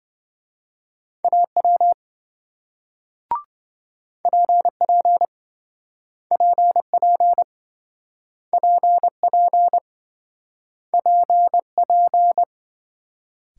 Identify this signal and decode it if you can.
Morse code, heard as a keyed tone, about 28 wpm, 700 Hz. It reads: AW PP PP PP PP